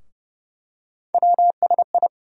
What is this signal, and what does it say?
Morse code, heard as a keyed tone, about 30 wpm, 700 Hz. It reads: WSI
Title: USA Call Sign Postfixes - 3 Letters 30wpm